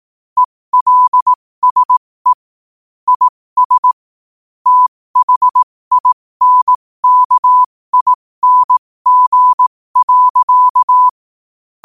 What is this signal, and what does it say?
Morse code, heard as a keyed tone, about 18 wpm, spaced as for 15 wpm, 1000 Hz. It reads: ELSE IS THINKING.
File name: qotdfiles / 15wpm